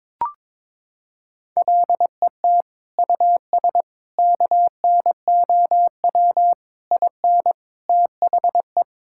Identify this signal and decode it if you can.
Morse code, heard as a keyed tone, about 22 wpm, 700 Hz. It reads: LET US KNOW IN THE
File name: Sentences with 7-Word Phrases - 22wpm